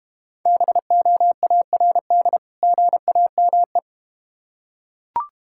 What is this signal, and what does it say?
Morse code, heard as a keyed tone, about 31 wpm, 700 Hz. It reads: BOARD GAME
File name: practice / 25wpm / Difficult Words - Speed-Racing with Hints - 25wpm